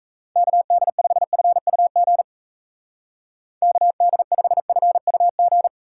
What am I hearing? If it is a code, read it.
Morse code, heard as a keyed tone, about 37 wpm, 700 Hz. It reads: KD5FUG KD5FUG